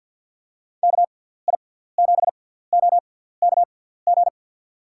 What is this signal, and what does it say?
Morse code, heard as a keyed tone, about 47 wpm, spaced as for 17 wpm, 700 Hz. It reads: KI7OKG
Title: USA Call Signs - 6 Characters - 6x Character Spacing - 50wpm